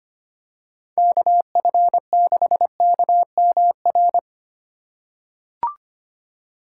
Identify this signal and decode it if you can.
Morse code, heard as a keyed tone, about 25 wpm, 700 Hz. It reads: KF6KMR
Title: USA Call Signs - 6 Characters - 3x Character Spacing - 25wpm